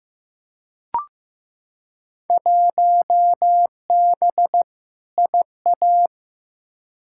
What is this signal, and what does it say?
Morse code, heard as a keyed tone, about 15 wpm, 700 Hz. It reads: 1B IA
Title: Field Day Exchanges Encore - 15wpm